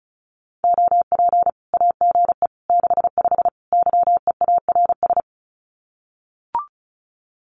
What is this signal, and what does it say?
Morse code, heard as a keyed tone, about 34 wpm, 700 Hz. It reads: OP AGE 65 YEARS
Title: Common QSO Phrases 35wpm